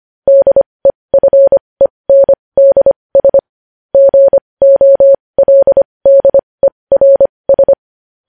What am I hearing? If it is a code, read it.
Morse code, heard as a keyed tone, about 25 wpm, spaced as for 20 wpm, 550 Hz. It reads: DEFENDS GOLDERS
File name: podcast / News headlines-1x-25x20